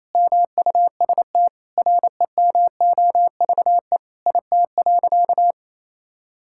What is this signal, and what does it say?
Morse code, heard as a keyed tone, about 28 wpm, 700 Hz. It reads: MUST REMOVE IT.
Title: Sentences from Top 1000 Words - Review Part 5 - 28wpm